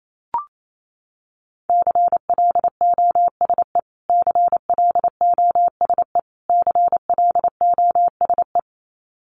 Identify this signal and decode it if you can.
Morse code, heard as a keyed tone, about 28 wpm, 700 Hz. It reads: CLOSE CLOSE CLOSE
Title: Top 300 Words Repeated 28wpm